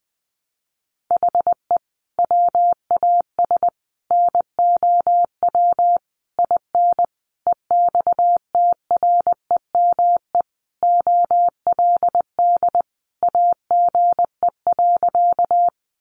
Morse code, heard as a keyed tone, about 20 wpm, 700 Hz. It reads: HE WAS NOW IN EXTREME OLD AGE.
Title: Sentences from Top 1300 Words 20wpm